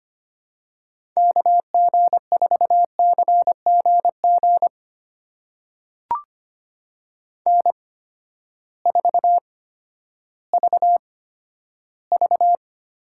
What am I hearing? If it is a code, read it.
Morse code, heard as a keyed tone, about 25 wpm, spaced as for 12 wpm, 700 Hz. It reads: KG4CGG N4VV
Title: USA Call Signs - 8x Character Spacing - 25wpm